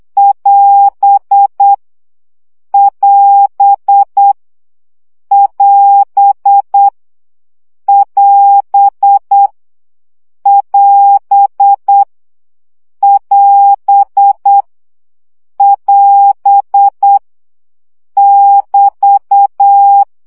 Morse code, heard as a keyed tone, about 8 wpm, 800 Hz. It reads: <AS> <AS> <AS> <AS> <AS> <AS> <AS> <BT>
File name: CW gyakorlas as 7 baud